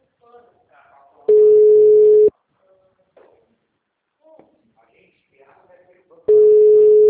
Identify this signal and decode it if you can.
Ringback tone (Europe, 425 Hz), with speech.
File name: out- -2005- - - .26014